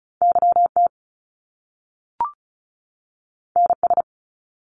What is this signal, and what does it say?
Morse code, heard as a keyed tone, about 36 wpm, 700 Hz. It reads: YT NS
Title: Canadian Provinces and Territories 35wpm